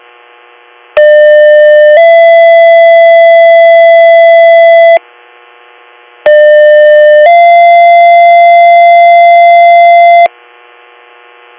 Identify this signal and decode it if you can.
Two-tone page, heard as a one-second tone, then a three-second tone, about 600 then 670 Hz, about 600 then 690 Hz. It is heard from 1.0 to 10.3 s.